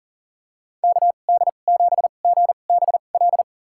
Morse code, heard as a keyed tone, about 40 wpm, spaced as for 29 wpm, 700 Hz. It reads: KD7GBL
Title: USA Call Signs - 6 Characters - 2x Character Spacing - 40wpm